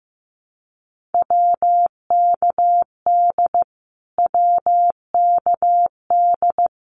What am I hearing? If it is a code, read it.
Morse code, heard as a keyed tone, about 15 wpm, 700 Hz. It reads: WKD WKD
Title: Abbreviations Repeated 3x - 15wpm